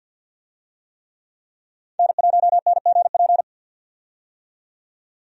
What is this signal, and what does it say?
Morse code, heard as a keyed tone, about 46 wpm, 700 Hz. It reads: N1NGP
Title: USA Call Signs - 5 Characters - 50wpm